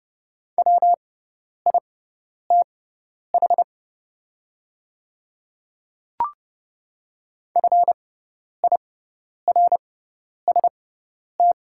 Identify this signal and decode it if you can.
Morse code, heard as a keyed tone, about 31 wpm, spaced as for 10 wpm, 700 Hz. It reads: WITH FIRST
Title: 6x Character Spacing - Top 100 Words - 30wpm